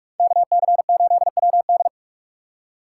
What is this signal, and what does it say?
Morse code, heard as a keyed tone, about 58 wpm, 700 Hz. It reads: KC8WD